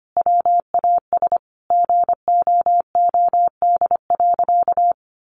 Morse code, heard as a keyed tone, about 25 wpm, 700 Hz. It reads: WAS GOOD.
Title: Sentences from Top 1100 Words 25wpm